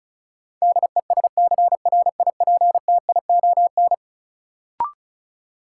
Morse code, heard as a keyed tone, about 35 wpm, 700 Hz. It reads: DESCRIPTION
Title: Sentences from Top 1300 Words - New Word Drill - 35wpm